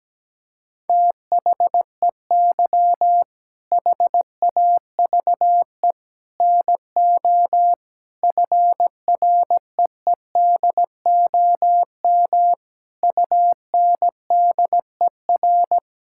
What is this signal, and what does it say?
Morse code, heard as a keyed tone, about 17 wpm, 700 Hz. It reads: THEY HAVE NO FREEDOM UNDER
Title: Sentences from Top 1000 Words - Review Part 5 - 17wpm